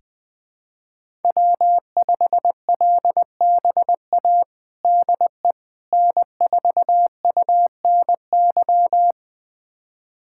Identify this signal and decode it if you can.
Morse code, heard as a keyed tone, about 20 wpm, 700 Hz. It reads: W5LBA DE N4UNY
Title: USA Callsign DE Callsign - 5 Characters - 20wpm